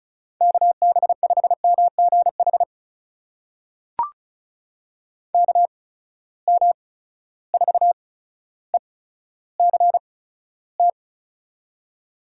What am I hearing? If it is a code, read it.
Morse code, heard as a keyed tone, about 36 wpm, spaced as for 15 wpm, 700 Hz. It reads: KB5MGH KM4ECT
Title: USA Call Signs - 8x Character Spacing - 35wpm